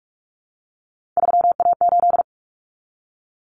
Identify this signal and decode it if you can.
Morse code, heard as a keyed tone, about 44 wpm, 700 Hz. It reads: 3A8
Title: Common Cut Numbers - 3-Digit 45wpm